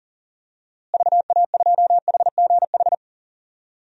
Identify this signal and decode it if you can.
Morse code, heard as a keyed tone, about 40 wpm, 700 Hz. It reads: VA2HGH
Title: DXCC-100 Call Signs - 6 Characters - Repeated 2x - 40wpm